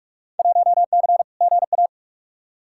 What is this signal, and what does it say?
Morse code, heard as a keyed tone, about 46 wpm, 700 Hz. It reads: 1C GA
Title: Field Day Exchanges Encore - 45wpm